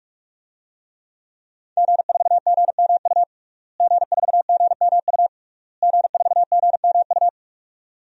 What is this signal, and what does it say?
Morse code, heard as a keyed tone, about 42 wpm, 700 Hz. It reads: G4GMU G4GMU G4GMU
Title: DXCC-100 Call Signs - 5 Characters - Repeated 3x - 45wpm